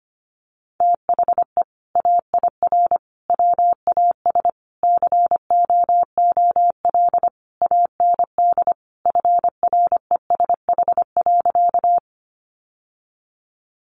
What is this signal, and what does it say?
Morse code, heard as a keyed tone, about 25 wpm, 700 Hz. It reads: THE AIR WAS COOL AND FRESH.
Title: Sentences from Top 1200 Words 25wpm